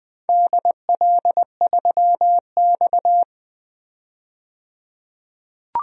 Morse code, heard as a keyed tone, about 20 wpm, 700 Hz. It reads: DL3X
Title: DXCC-100 Call Signs - 4 Characters - 20wpm